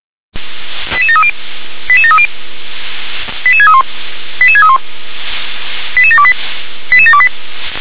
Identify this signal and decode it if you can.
Five-tone selective call (ZVEI), heard as quick runs of five tones, tones of about 70 ms. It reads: 88730 88730 88631 88631 88728 88728